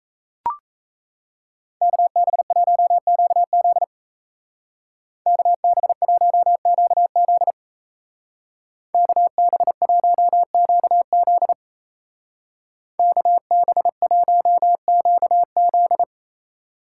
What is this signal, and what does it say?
Morse code, heard as a keyed tone, about 34 wpm, 700 Hz. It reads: KB1QZ KB1QZ KB1QZ KB1QZ
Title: Speed Racing - USA Call Signs 28wpm